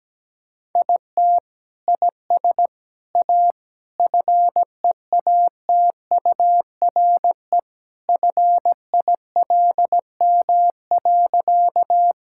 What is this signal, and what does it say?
Morse code, heard as a keyed tone, about 17 wpm, 700 Hz. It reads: IT IS A FEATURE FILM.
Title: Sentences from Top 1000 Words - Review Part 5 - 17wpm